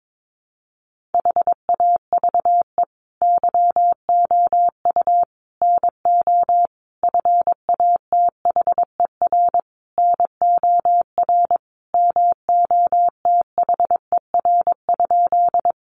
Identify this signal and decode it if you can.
Morse code, heard as a keyed tone, about 22 wpm, 700 Hz. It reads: HAVE YOU NO FATHER NOR MOTHER?
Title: Sentences from Top 1500 Words 22wpm